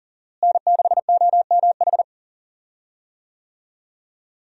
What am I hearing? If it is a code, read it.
Morse code, heard as a keyed tone, about 37 wpm, 700 Hz. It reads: N6OMH
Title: USA Call Signs - 5 Characters - Repeated 3x - 40wpm